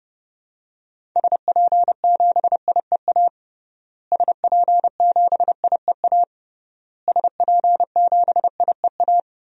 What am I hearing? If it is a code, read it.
Morse code, heard as a keyed tone, about 30 wpm, 700 Hz. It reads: SP7IEA SP7IEA SP7IEA
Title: DXCC-100 Call Signs - 6 Characters - Repeated 3x - 30wpm